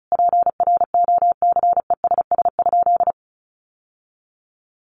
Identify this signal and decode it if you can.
Morse code, heard as a keyed tone, about 34 wpm, 700 Hz. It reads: PROCESS?
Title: Sentences from Top 1400 Words 35wpm